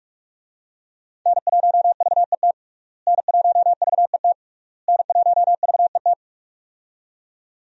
Morse code, heard as a keyed tone, about 46 wpm, 700 Hz. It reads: N1VET N1VET N1VET
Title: USA Call Signs - 5 Characters - Repeated 3x - 45wpm